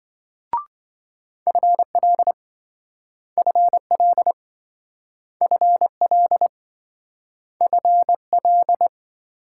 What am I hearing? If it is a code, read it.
Morse code, heard as a keyed tone, about 27 wpm, 700 Hz. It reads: FL FL FL FL